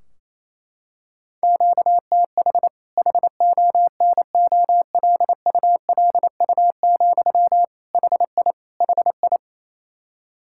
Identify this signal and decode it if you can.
Morse code, heard as a keyed tone, about 28 wpm, 700 Hz. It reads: QTH HONOLULU, HI HI
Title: Practice QSO - Exchange 1 - Beginning - 28wpm